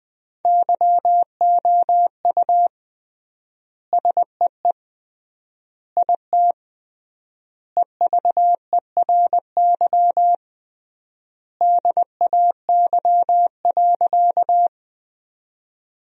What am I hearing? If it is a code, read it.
Morse code, heard as a keyed tone, about 21 wpm, 700 Hz. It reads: YOU SEE IT EVERY DAY.
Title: 3x Word-Spacing - Sentences from Top 200 Words - 20wpm